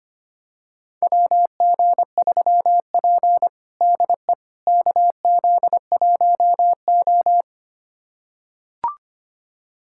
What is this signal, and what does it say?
Morse code, heard as a keyed tone, about 25 wpm, 700 Hz. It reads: WG3P DE KZ1O
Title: USA Callsign DE Callsign - 4 Characters - 25wpm